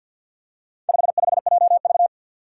Morse code, heard as a keyed tone, about 53 wpm, 700 Hz. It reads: 55JV
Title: Four Challenging Letter-Number Combinations 50wpm